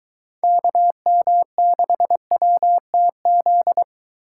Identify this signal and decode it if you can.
Morse code, heard as a keyed tone, about 23 wpm, 700 Hz. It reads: KM6WTZ